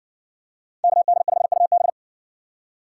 Morse code, heard as a keyed tone, about 57 wpm, 700 Hz. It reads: KD5UB